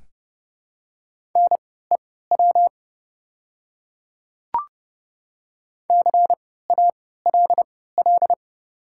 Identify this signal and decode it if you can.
Morse code, heard as a keyed tone, about 30 wpm, spaced as for 17 wpm, 700 Hz. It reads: NEW CALL
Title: 3x Character Spacing - Top 500 Words - 30wpm